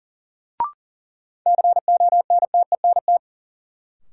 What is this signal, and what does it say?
Morse code, heard as a keyed tone, about 37 wpm, 700 Hz. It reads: CONTENT
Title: Sentences from Top 900 Words - New Word Drill - 40wpm